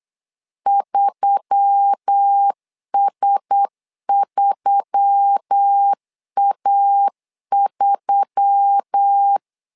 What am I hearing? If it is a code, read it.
Morse code, heard as a keyed tone, about 8 wpm, 800 Hz. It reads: 3S3A3
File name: LEZIONI / 4 Q1S3 8wpm 3037